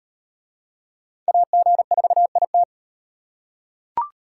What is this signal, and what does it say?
Morse code, heard as a keyed tone, about 37 wpm, 700 Hz. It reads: AG4IT